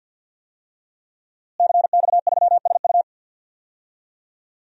Morse code, heard as a keyed tone, about 47 wpm, 700 Hz. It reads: CX3SU
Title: DXCC-100 Call Signs - 5 Characters - Repeated 3x - 50wpm